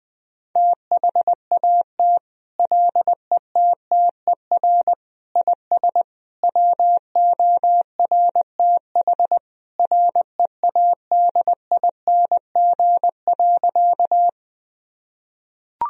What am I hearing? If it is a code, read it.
Morse code, heard as a keyed tone, about 21 wpm, 700 Hz. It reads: THAT LETTER IS WORTH READING.
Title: Sentences from Top 1000 Words - Review Part 3 - 20wpm